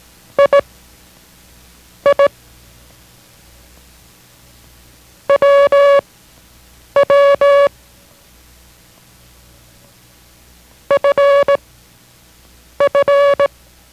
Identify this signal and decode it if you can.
Morse code, heard as a keyed tone, about 18 wpm, spaced as for 6 wpm, 560 Hz. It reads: II WW FF